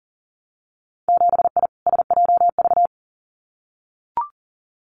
Morse code, heard as a keyed tone, about 40 wpm, 700 Hz. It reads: 7I SJV
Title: Winter Field Day Exchanges Encore 40wpm